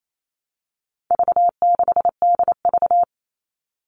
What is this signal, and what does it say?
Morse code, heard as a keyed tone, about 28 wpm, 700 Hz. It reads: V6DV
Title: All Cut Numbers - 4-Digit 28wpm